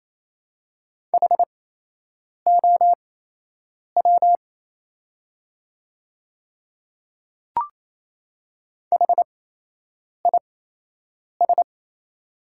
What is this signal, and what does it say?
Morse code, heard as a keyed tone, about 28 wpm, spaced as for 8 wpm, 700 Hz. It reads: HOW HIS
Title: 8x Character Spacing - Top 100 Words - 28wpm